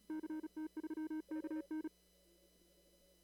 Morse code, heard as a keyed tone, about 37 wpm, 310 Hz. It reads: CT3KN
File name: CW / kt5j wpx cw-op-k5pi